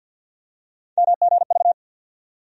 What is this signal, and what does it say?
Morse code, heard as a keyed tone, about 52 wpm, 700 Hz. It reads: MGV
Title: USA Call Sign Postfixes - 3 Letters 50wpm